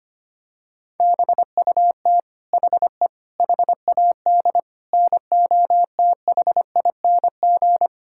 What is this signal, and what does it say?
Morse code, heard as a keyed tone, about 25 wpm, 700 Hz. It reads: BUT HE HAD NOTHING
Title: Sentences from Top 1600 Words 25wpm